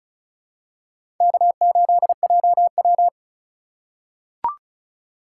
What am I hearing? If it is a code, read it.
Morse code, heard as a keyed tone, about 36 wpm, 700 Hz. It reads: K8JW
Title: USA Call Signs - 4 Characters - 3x Character Spacing - 35wpm